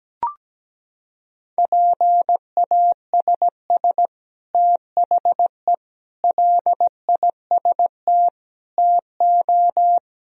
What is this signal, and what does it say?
Morse code, heard as a keyed tone, about 17 wpm, 700 Hz. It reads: PASS THE LIST TO
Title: Sentences from Top 1000 Words - Review Part 2 - 17wpm